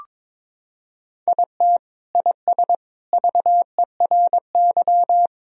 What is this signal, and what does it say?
Morse code, heard as a keyed tone, about 22 wpm, 700 Hz. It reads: IT IS VERY
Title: Sentences from Top 1000 Words - Review Part 2 - 22wpm